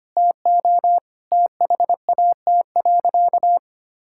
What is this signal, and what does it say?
Morse code, heard as a keyed tone, about 25 wpm, 700 Hz. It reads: TO THAT.
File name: practice / 25wpm / Sentences with 3-Word Phrases - 25wpm